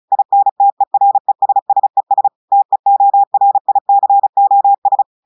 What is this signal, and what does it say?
Morse code, heard as a keyed tone, about 36 wpm, 800 Hz. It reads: INTERESSES TEORICOS
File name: audios / portugais / portugais texte en clair 1 vitesse 35 mots minutes